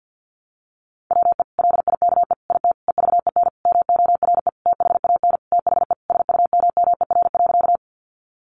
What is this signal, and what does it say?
Morse code, heard as a keyed tone, about 50 wpm, 700 Hz. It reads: WE LIKE IT EVEN MORE THAN THE SUMMER.
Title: Sentences from Top 1100 Words 50wpm